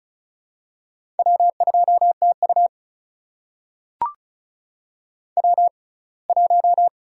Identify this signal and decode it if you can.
Morse code, heard as a keyed tone, about 36 wpm, spaced as for 24 wpm, 700 Hz. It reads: W2TU W1
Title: USA Call Signs - 4 Characters - 6x Character Spacing - 35wpm